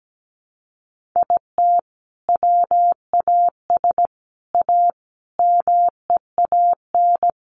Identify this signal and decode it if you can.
Morse code, heard as a keyed tone, about 17 wpm, 700 Hz. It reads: IT WAS A MEAN
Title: Sentences from Top 1500 Words 17wpm